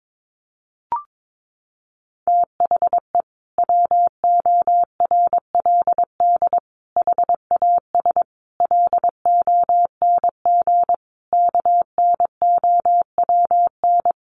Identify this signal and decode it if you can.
Morse code, heard as a keyed tone, about 22 wpm, 700 Hz. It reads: THE WORLD HAS LONG KNOWN
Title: Sentences from Top 1200 Words 22wpm